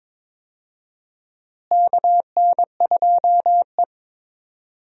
Morse code, heard as a keyed tone, about 21 wpm, 700 Hz. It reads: KN2E
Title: USA Call Signs - 4 Characters - Repeated 3x - 22wpm